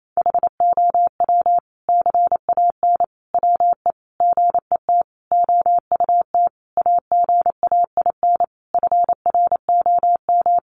Morse code, heard as a keyed tone, about 28 wpm, 700 Hz. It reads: HOW CAN WE GET OUT AGAIN FROM
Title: Sentences from Top 1200 Words 28wpm